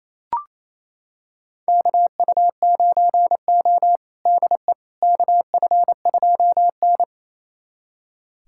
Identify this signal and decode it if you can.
Morse code, heard as a keyed tone, about 28 wpm, 700 Hz. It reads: KU9O DE KF2N
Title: USA Callsign DE Callsign - 4 Characters - 28wpm